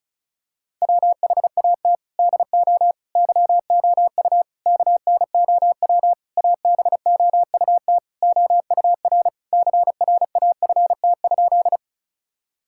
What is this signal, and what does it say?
Morse code, heard as a keyed tone, about 36 wpm, 700 Hz. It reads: WHAT DO YOU KNOW ABOUT OUR CRAFT?
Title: Sentences from Top 1000 Words - Review Part 6 - 35wpm